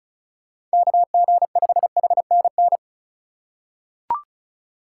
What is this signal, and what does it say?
Morse code, heard as a keyed tone, about 34 wpm, 700 Hz. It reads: KG5HNN
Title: USA Call Signs - 6 Characters - 4x Character Spacing - 35wpm